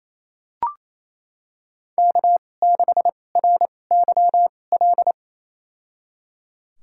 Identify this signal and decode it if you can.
Morse code, heard as a keyed tone, about 27 wpm, spaced as for 20 wpm, 700 Hz. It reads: K6RYL
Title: USA Call Signs - 2x Character Spacing - 28wpm